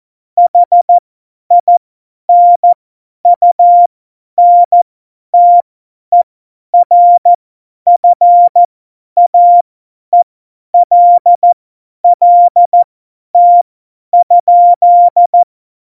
Morse code, heard as a keyed tone, about 14 wpm, spaced as for 10 wpm, 700 Hz. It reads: HINUNTERFAELLT?